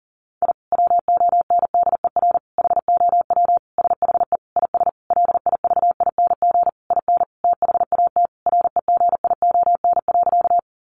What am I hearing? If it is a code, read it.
Morse code, heard as a keyed tone, about 40 wpm, 700 Hz. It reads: I WONDER HOW SHE IS LIVING IN THAT REGION.
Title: Sentences from Top 1500 Words 40wpm